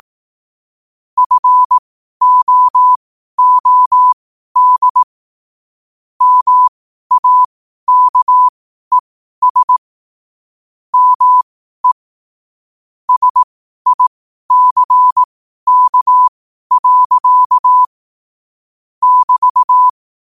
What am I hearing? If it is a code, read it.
Morse code, heard as a keyed tone, about 18 wpm, spaced as for 12 wpm, 1000 Hz. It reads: FOOD MAKES ME SICK. <BT>
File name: qotdfiles / 12wpm